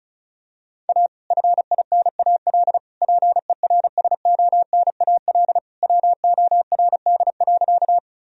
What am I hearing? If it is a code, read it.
Morse code, heard as a keyed tone, about 36 wpm, 700 Hz. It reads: A FINAL PERSONAL WORD.
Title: Sentences from Top 1000 Words - Review Part 4 - 35wpm